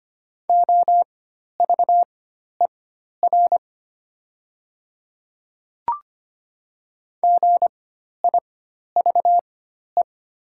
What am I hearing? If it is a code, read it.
Morse code, heard as a keyed tone, about 25 wpm, spaced as for 12 wpm, 700 Hz. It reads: OVER GIVE
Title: 4x Character Spacing - Top 200 Words - 25wpm